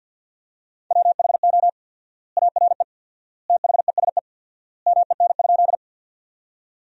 Morse code, heard as a keyed tone, about 47 wpm, 700 Hz. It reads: WHO ARE THESE MEN?
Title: 4x Word-Spacing - Sentences from Top 200 Words - 50wpm